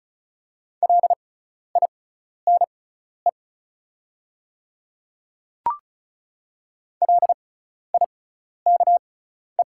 Morse code, heard as a keyed tone, about 34 wpm, spaced as for 12 wpm, 690 Hz. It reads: LINE LIKE
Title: 6x Character Spacing - Top 200 Words - 35wpm